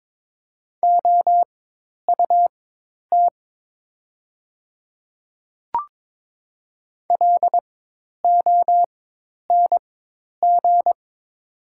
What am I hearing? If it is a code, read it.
Morse code, heard as a keyed tone, about 22 wpm, spaced as for 10 wpm, 700 Hz. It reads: OUT LONG
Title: 4x Character Spacing - Top 100 Words - 22wpm